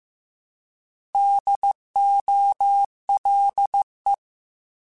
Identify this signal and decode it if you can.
Morse code, heard as a keyed tone, about 15 wpm, 800 Hz. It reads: DOLE
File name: Alphabetical / Alphabetical mp3s / Ds @13